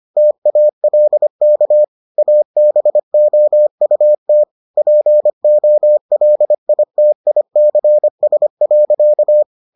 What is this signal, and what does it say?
Morse code, heard as a keyed tone, about 25 wpm, 600 Hz. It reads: TALK ABOUT POLITICS.